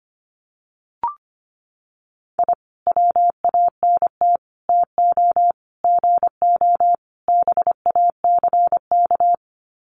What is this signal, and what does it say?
Morse code, heard as a keyed tone, about 25 wpm, 700 Hz. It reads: I WANT TO GO BACK
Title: Sets of 5 Words Encore 25wpm